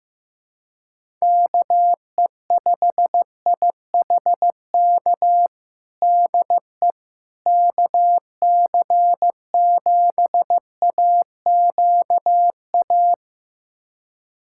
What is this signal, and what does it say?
Morse code, heard as a keyed tone, about 15 wpm, 700 Hz. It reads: KE5IHK DE KC7AQA
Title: USA Callsign DE Callsign - 6 Characters - 15wpm